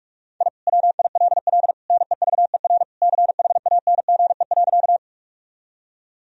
Morse code, heard as a keyed tone, about 46 wpm, 700 Hz. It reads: I WILL NEVER CHANGE.
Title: Sentences from Top 1000 Words - Review Part 1 - 45wpm